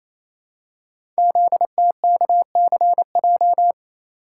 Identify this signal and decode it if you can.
Morse code, heard as a keyed tone, about 29 wpm, 700 Hz. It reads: ZTKCJ
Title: Code Groups - 5 Letters - 28wpm